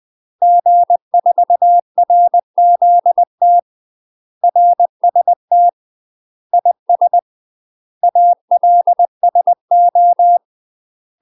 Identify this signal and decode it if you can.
Morse code, heard as a keyed tone, about 20 wpm, 700 Hz. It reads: G4RZT RST IS ALSO
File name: UK MASTER QSOs TRAINING / 20WPM / UK MASTER QSOs 0068